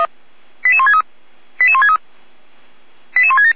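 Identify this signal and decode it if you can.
Five-tone selective call (ZVEI), heard as quick runs of five tones, tones of about 70 ms. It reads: 88163 88163 88167
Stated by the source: ZVEI five-tone selective call received by SDR, on the server of Feuerwehr Orlamünde (fire brigade).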